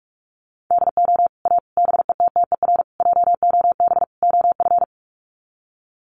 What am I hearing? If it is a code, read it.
Morse code, heard as a keyed tone, about 44 wpm, 700 Hz. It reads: DO A BETTER JOB OF
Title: Sets of 5 Words Encore2 45wpm